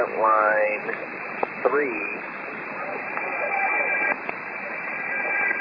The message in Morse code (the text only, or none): none